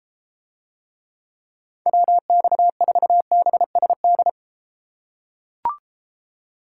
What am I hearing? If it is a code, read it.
Morse code, heard as a keyed tone, about 32 wpm, 700 Hz. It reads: WX4BSD